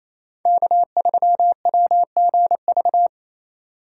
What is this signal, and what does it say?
Morse code, heard as a keyed tone, about 34 wpm, 700 Hz. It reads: K3WGV
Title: Speed Racing - USA 1x3 Call Signs 28wpm